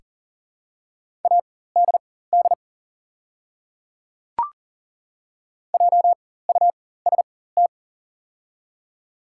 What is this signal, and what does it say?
Morse code, heard as a keyed tone, about 41 wpm, spaced as for 19 wpm, 700 Hz. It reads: ADD JUST
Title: 4x Character Spacing - Top 200 Words - 40wpm